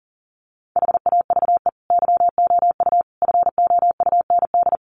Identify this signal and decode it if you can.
Morse code, heard as a keyed tone, about 37 wpm, 700 Hz. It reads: HAVE YOU FOUND
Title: Sentences from Top 300 Words 40wpm